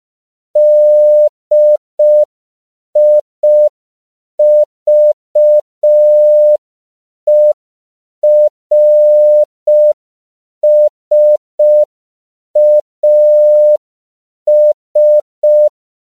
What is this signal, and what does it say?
Morse code, heard as a keyed tone, about 5 wpm, 600 Hz. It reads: DIVERSAS
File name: CW / 5PPM